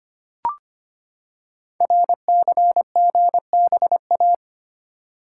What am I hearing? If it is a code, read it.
Morse code, heard as a keyed tone, about 25 wpm, 700 Hz. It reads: RCGBA
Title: Code Groups - 5 Letters - 25wpm